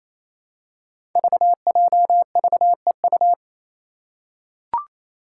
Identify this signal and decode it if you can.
Morse code, heard as a keyed tone, about 28 wpm, 700 Hz. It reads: VJVEU
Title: Five Challenging Letter-Number Combinations 28wpm